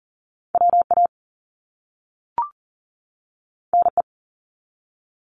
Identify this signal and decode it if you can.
Morse code, heard as a keyed tone, about 38 wpm, 700 Hz. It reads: WA NE
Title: US States 40wpm